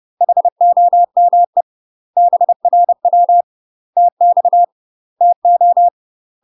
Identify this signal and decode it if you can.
Morse code, heard as a keyed tone, about 30 wpm, 700 Hz. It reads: HOME BRW TX TO
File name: UK MASTER QSOs TRAINING / 30WPM / UK MASTER QSOs 0043